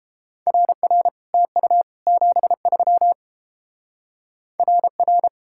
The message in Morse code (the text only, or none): RR TU 73 RR